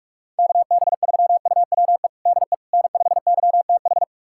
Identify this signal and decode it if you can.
Morse code, heard as a keyed tone, about 46 wpm, 700 Hz. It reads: KB3UWE DE N5YTH
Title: USA Callsign DE Callsign 45wpm